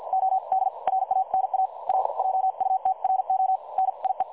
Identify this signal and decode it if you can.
Morse code, heard as a keyed tone, about 51 wpm, spaced as for 44 wpm, 760 Hz. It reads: OD RANA A JUTRO NIE
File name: dlazaawansowanych / nagraniazpasma / websdr recording 3565.0kHz